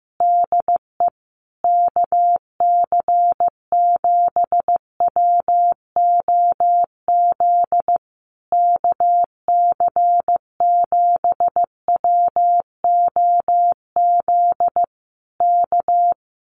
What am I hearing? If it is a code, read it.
Morse code, heard as a keyed tone, about 15 wpm, 700 Hz. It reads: DE KC7WOZ KC7WOZ K
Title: Common QSO Phrases Encore 15wpm